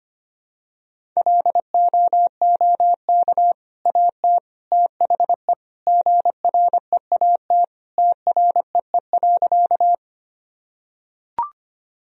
Morse code, heard as a keyed tone, about 25 wpm, 700 Hz. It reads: LOOK AT THE GREAT TREE.
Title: Sentences from Top 1000 Words - Review Part 1 - 25wpm